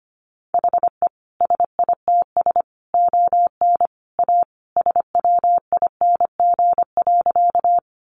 Morse code, heard as a keyed tone, about 25 wpm, 700 Hz. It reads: HE SITS ON A SWING.
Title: Additional Easy Phrases 25wpm